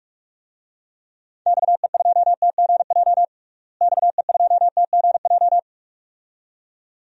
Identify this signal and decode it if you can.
Morse code, heard as a keyed tone, about 41 wpm, 700 Hz. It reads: XE2TGJ XE2TGJ
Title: DXCC-100 Call Signs - 6 Characters - Repeated 2x - 45wpm